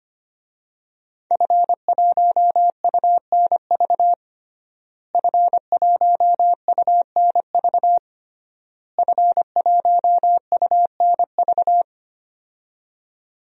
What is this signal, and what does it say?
Morse code, heard as a keyed tone, about 25 wpm, 700 Hz. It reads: F1UNV F1UNV F1UNV
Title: DXCC-100 Call Signs - 5 Characters - Repeated 3x - 25wpm